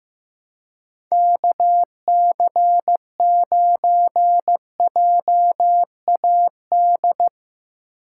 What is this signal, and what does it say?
Morse code, heard as a keyed tone, about 15 wpm, 700 Hz. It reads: KC9JAD
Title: USA Call Signs - 2x Character Spacing - 15wpm